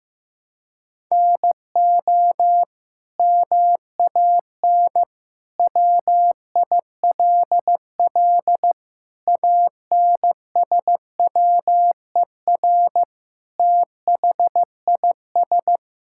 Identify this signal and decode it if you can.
Morse code, heard as a keyed tone, about 15 wpm, 700 Hz. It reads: NO MAN WILL ANSWER THIS
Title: Sentences from Top 1000 Words - Review Part 6 - 15wpm